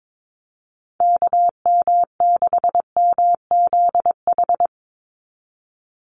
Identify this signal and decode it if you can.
Morse code, heard as a keyed tone, about 22 wpm, 700 Hz. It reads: KM6MZH
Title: USA Call Signs 22wpm